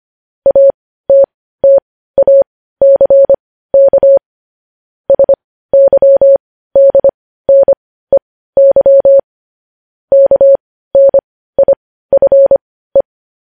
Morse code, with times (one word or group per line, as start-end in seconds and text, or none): 0.5-4.2 ATTACK
5.1-9.2 SYDNEY
10.1-13.0 KNIFE